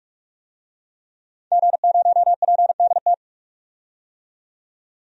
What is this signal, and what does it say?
Morse code, heard as a keyed tone, about 46 wpm, 700 Hz. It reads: G0PDT
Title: DXCC-100 Call Signs - 5 Characters - 45wpm